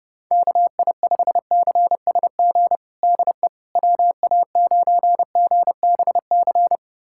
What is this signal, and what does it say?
Morse code, heard as a keyed tone, about 32 wpm, 700 Hz. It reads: KI5CSG DE WA9GBC
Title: USA Callsign DE Callsign - 6 Characters - 30wpm